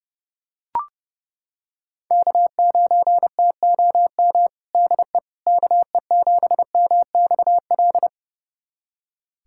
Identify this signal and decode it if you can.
Morse code, heard as a keyed tone, about 30 wpm, 700 Hz. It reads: K9TOM DE KE7MXL
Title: USA Callsign DE Callsign 30wpm